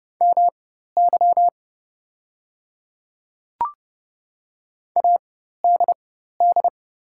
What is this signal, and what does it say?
Morse code, heard as a keyed tone, about 30 wpm, spaced as for 14 wpm, 700 Hz. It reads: MY ADD
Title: 4x Character Spacing - Top 300 Words - 30wpm